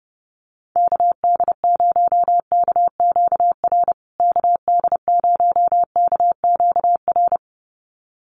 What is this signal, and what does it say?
Morse code, heard as a keyed tone, about 30 wpm, 700 Hz. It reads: KD0KQR KD0KQR
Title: USA Call Signs Repeated 2x 30wpm